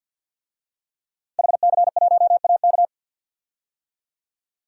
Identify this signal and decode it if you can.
Morse code, heard as a keyed tone, about 46 wpm, 700 Hz. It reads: HC1AK
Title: DXCC-100 Call Signs - 5 Characters - Repeated 2x - 50wpm